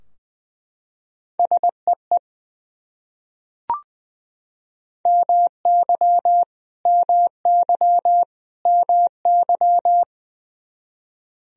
Morse code, heard as a keyed tone, about 20 wpm, 700 Hz. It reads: SEE MY MY MY
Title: Top 100 Words Repeated 20wpm